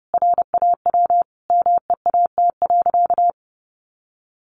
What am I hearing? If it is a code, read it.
Morse code, heard as a keyed tone, about 30 wpm, 700 Hz. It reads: RAW MEAT.
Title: Sentences from Top 1500 Words 30wpm